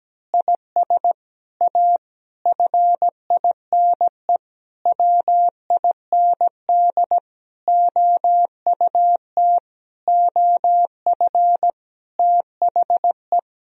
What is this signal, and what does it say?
Morse code, heard as a keyed tone, about 17 wpm, 700 Hz. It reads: IS A FINE WIND OUT OF THE
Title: Sentences from Top 1000 Words - Review Part 2 - 17wpm